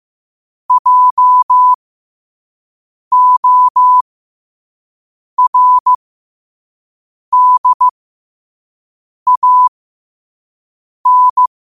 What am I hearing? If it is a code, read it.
Morse code, heard as a keyed tone, about 15 wpm, spaced as for 5 wpm, 1000 Hz. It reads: JORDAN
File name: qotdfiles / 5wpm